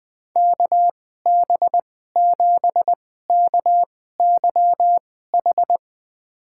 Morse code, heard as a keyed tone, about 20 wpm, spaced as for 14 wpm, 700 Hz. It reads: KB7KYH
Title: USA Call Signs - 2x Character Spacing - 20wpm